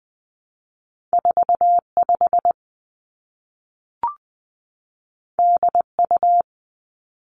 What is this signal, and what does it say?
Morse code, heard as a keyed tone, about 19 wpm, 700 Hz. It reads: 4 5 D U
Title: Two Challenging Letter-Number Combinations 20wpm